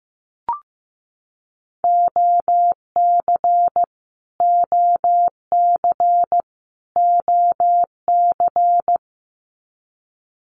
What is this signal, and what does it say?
Morse code, heard as a keyed tone, about 15 wpm, 700 Hz. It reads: OC OC OC
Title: Abbreviations Repeated 3x - 15wpm